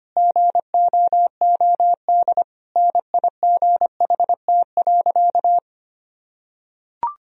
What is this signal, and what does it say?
Morse code, heard as a keyed tone, about 25 wpm, 700 Hz. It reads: GOOD NIGHT.
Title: Sentences from Top 1000 Words - Review Part 4 - 25wpm